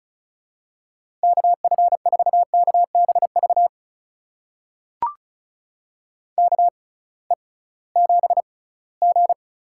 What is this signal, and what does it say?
Morse code, heard as a keyed tone, about 36 wpm, spaced as for 21 wpm, 700 Hz. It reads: KF4KBV KE7G
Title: USA Call Signs - 6x Character Spacing - 35wpm